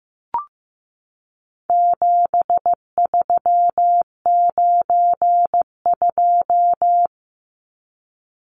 Morse code, heard as a keyed tone, about 15 wpm, 700 Hz. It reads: 7392
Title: Numbers - 4-Digit 15wpm